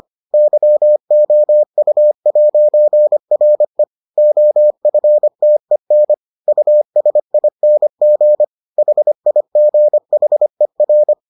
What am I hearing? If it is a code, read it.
Morse code, heard as a keyed tone, about 25 wpm, 600 Hz. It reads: YOU'RE OFTEN USING HIGHER